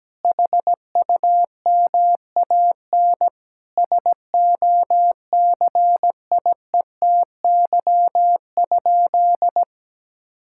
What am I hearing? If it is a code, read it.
Morse code, heard as a keyed tone, about 17 wpm, 700 Hz. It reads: HUMAN SOCIETY?
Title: Sentences from Top 1000 Words - Review Part 6 - 17wpm